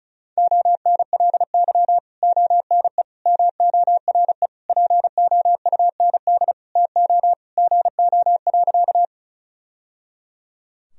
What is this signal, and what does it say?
Morse code, heard as a keyed tone, about 36 wpm, 700 Hz. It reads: ONLY ONE MORE POUND TO GO.
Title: Sentences from Top 1000 Words - Review Part 2 - 35wpm